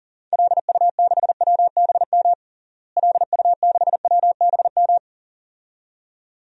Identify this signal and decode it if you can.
Morse code, heard as a keyed tone, about 40 wpm, 700 Hz. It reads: LU6WBM LU6WBM
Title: DXCC-100 Call Signs - 6 Characters - Repeated 2x - 40wpm